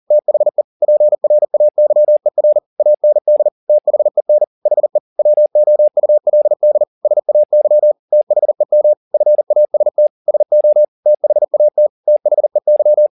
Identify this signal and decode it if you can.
Morse code, heard as a keyed tone, about 40 wpm, 600 Hz. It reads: THE PRAYER AND THEN HE WOULD SAY THEM FAST SO THAT THEY